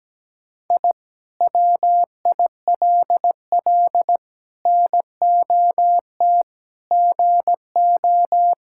Morse code, heard as a keyed tone, about 17 wpm, 700 Hz. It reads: I WILL NOT GO
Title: Sentences from Top 1000 Words - Review Part 6 - 17wpm